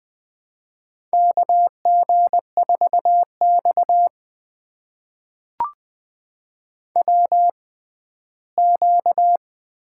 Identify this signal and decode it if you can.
Morse code, heard as a keyed tone, about 20 wpm, spaced as for 14 wpm, 700 Hz. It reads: KG4X WQ
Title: USA Call Signs - 4 Characters - 6x Character Spacing - 20wpm